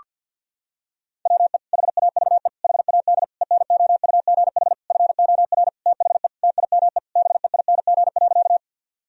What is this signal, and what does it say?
Morse code, heard as a keyed tone, about 50 wpm, 700 Hz. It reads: WE HAVE HAD ENOUGH FOR THE TIME BEING.
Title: Sentences from Top 1000 Words - Review Part 3 - 50wpm